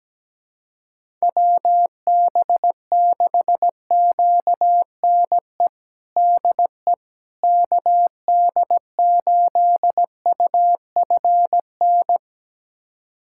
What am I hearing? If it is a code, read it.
Morse code, heard as a keyed tone, about 17 wpm, 700 Hz. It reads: WB6QNE DE KD8UFN
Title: USA Callsign DE Callsign - 6 Characters - 17wpm